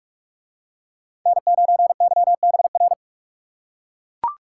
Morse code, heard as a keyed tone, about 41 wpm, 700 Hz. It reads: N9YBR